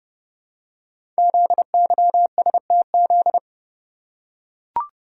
Morse code, heard as a keyed tone, about 28 wpm, 700 Hz. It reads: ZYSTZ
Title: Code Groups - 5 Letters - 30wpm